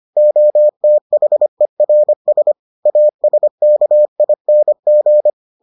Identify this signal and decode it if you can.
Morse code, heard as a keyed tone, about 25 wpm, 600 Hz. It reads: OTHERS ASKING